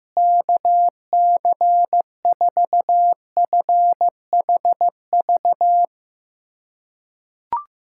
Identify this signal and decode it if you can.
Morse code, heard as a keyed tone, about 15 wpm, 700 Hz. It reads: KC4FHV
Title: USA Callsign DE Callsign - 6 Characters - 15wpm